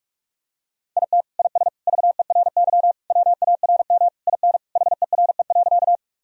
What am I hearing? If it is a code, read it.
Morse code, heard as a keyed tone, about 43 wpm, 700 Hz. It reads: IT IS VERY WARM IN HERE.
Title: Sentences from Top 1000 Words - Review Part 2 - 45wpm